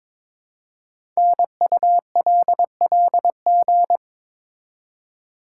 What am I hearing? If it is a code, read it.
Morse code, heard as a keyed tone, about 22 wpm, 700 Hz. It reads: NULLG
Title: Code Groups - 5 Letter:Numbers - 22wpm